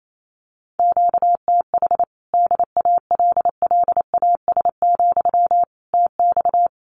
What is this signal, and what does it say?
Morse code, heard as a keyed tone, about 28 wpm, 700 Hz. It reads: QTH DALLAS, TX
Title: Common QSO Phrases Encore 28wpm